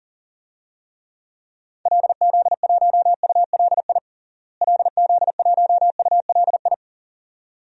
Morse code, heard as a keyed tone, about 40 wpm, 700 Hz. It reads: LZ1ULI LZ1ULI
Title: DXCC-100 Call Signs - 6 Characters - Repeated 2x - 40wpm